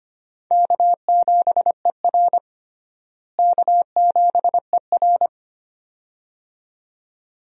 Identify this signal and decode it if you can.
Morse code, heard as a keyed tone, about 25 wpm, 700 Hz. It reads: K7ER K7ER
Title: USA Call Signs - 4 Characters - Repeated 2x - 25wpm